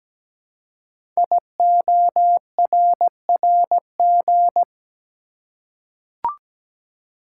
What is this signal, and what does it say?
Morse code, heard as a keyed tone, about 17 wpm, 700 Hz. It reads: IORRG
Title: Code Groups - 5 Letters - 17wpm